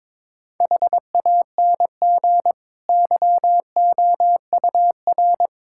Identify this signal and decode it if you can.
Morse code, heard as a keyed tone, about 22 wpm, 700 Hz. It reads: HANG YOUR
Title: Sentences from Top 1600 Words 22wpm